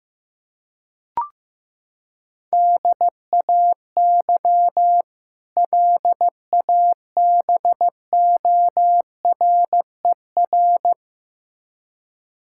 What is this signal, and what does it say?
Morse code, heard as a keyed tone, about 15 wpm, 700 Hz. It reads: DAY LABORER
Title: K1USN SST QSO Practice 15wpm